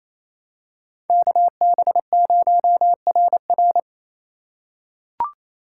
Morse code, heard as a keyed tone, about 28 wpm, 700 Hz. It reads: KB0RR
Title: USA Call Signs - 5 Characters - 2x Character Spacing - 28wpm